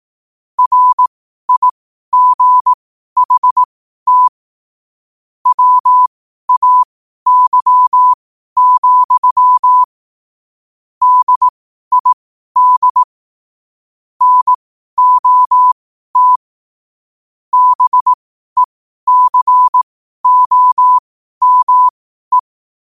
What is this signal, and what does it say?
Morse code, heard as a keyed tone, about 18 wpm, spaced as for 12 wpm, 1000 Hz. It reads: RIGHT WAY, DID NOT BECOME
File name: qotdfiles / 12wpm